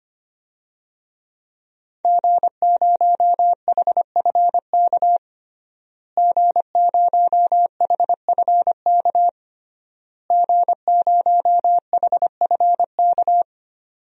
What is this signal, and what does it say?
Morse code, heard as a keyed tone, about 25 wpm, 700 Hz. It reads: G0HFK G0HFK G0HFK
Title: DXCC-100 Call Signs - 5 Characters - Repeated 3x - 25wpm